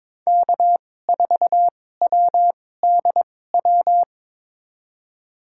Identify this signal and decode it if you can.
Morse code, heard as a keyed tone, about 22 wpm, spaced as for 16 wpm, 700 Hz. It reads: K4WDW
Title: USA Call Signs - 5 Characters - 2x Character Spacing - 22wpm